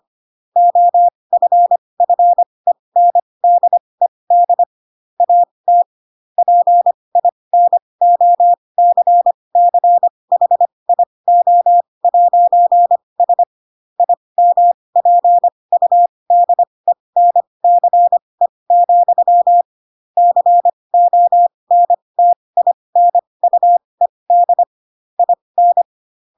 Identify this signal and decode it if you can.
Morse code, heard as a keyed tone, about 25 wpm, spaced as for 20 wpm, 700 Hz. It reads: OFFENDED AT PINOCCHIO'S IMPUDENCE, CONTINUED IN
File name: Pinocchio / Pinocchio ch4 15 25 ik1hge